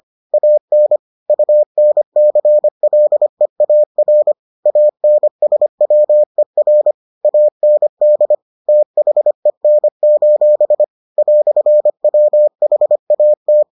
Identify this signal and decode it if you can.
Morse code, heard as a keyed tone, about 25 wpm, 600 Hz. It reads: AN UNCLEAR ANSWER AND THEN: "WHAT